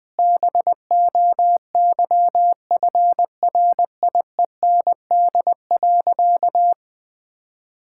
Morse code, heard as a keyed tone, about 19 wpm, 700 Hz. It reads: BOYFRIEND.
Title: Sentences from Top 1000 Words - Review Part 5 - 20wpm